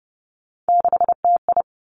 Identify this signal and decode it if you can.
Morse code, heard as a keyed tone, about 31 wpm, 700 Hz. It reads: 6TI
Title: Three Challenging Letter-Number Combinations 30wpm